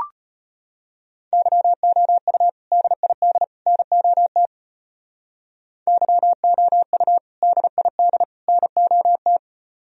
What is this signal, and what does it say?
Morse code, heard as a keyed tone, about 32 wpm, 700 Hz. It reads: YOU DID NOT YOU DID NOT